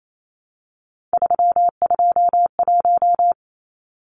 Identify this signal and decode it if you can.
Morse code, heard as a keyed tone, about 28 wpm, 700 Hz. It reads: 321
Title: Numbers - 3-Digit 28wpm